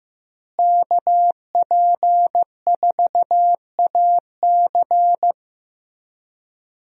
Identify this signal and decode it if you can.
Morse code, heard as a keyed tone, about 15 wpm, 700 Hz. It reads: KP4AC
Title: USA Call Signs - 5 Characters - Repeated 3x - 15wpm